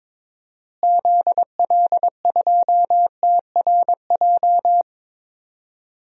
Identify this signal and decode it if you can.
Morse code, heard as a keyed tone, about 21 wpm, 700 Hz. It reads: ZL2TRJ
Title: DXCC-100 Call Signs - 6 Characters - Repeated 2x - 22wpm